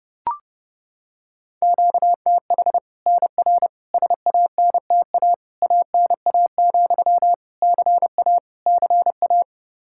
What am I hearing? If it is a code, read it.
Morse code, heard as a keyed tone, about 30 wpm, 700 Hz. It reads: QTH NR SANTA ANA, CA CA
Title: Practice QSO - Exchange 1 - Beginning - 30wpm